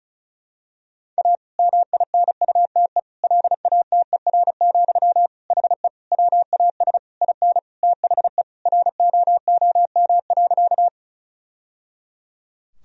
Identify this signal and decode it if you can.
Morse code, heard as a keyed tone, about 36 wpm, 700 Hz. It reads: A MINUTE LATER, HE WAS IN THE ROOM.
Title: Sentences from Top 1000 Words - Review Part 4 - 35wpm